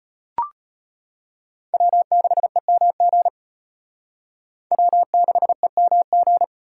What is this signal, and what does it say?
Morse code, heard as a keyed tone, about 34 wpm, 700 Hz. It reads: W6EMG W6EMG